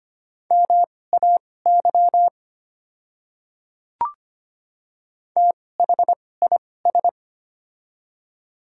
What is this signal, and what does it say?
Morse code, heard as a keyed tone, about 25 wpm, spaced as for 18 wpm, 690 Hz. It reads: MAY THIS
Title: 2x Character Spacing - Top 100 Words - 25wpm